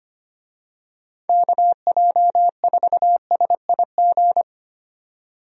Morse code, heard as a keyed tone, about 25 wpm, 700 Hz. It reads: KJ4SIG
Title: USA Call Signs - 6 Characters - 25wpm